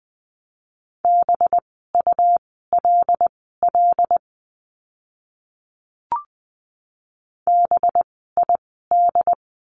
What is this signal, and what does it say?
Morse code, heard as a keyed tone, about 20 wpm, spaced as for 14 wpm, 700 Hz. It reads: BULL BID
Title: 185 Lesson 30f - Letter b - Words 20wpm